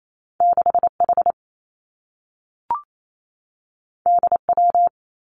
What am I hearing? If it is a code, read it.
Morse code, heard as a keyed tone, about 28 wpm, 700 Hz. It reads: 6 H D W
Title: Two Challenging Letter-Number Combinations 28wpm